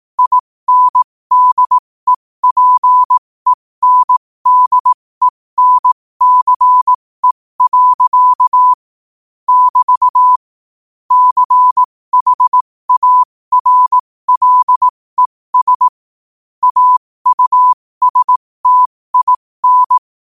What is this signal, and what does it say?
Morse code, heard as a keyed tone, about 18 wpm, spaced as for 15 wpm, 1000 Hz. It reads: INDEPENDENCE. <BT> CHARLES AUSTIN
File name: qotdfiles / 15wpm